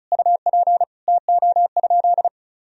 Morse code, heard as a keyed tone, about 36 wpm, 700 Hz. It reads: UP TO?
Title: Sentences from Top 100 Words and Abbreviations 35wpm